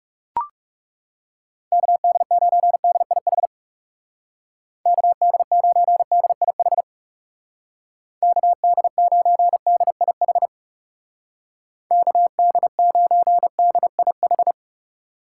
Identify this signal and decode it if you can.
Morse code, heard as a keyed tone, about 36 wpm, 700 Hz. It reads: KD9DIH KD9DIH KD9DIH KD9DIH
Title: Speed Racing - USA 2x3 Call Signs 30wpm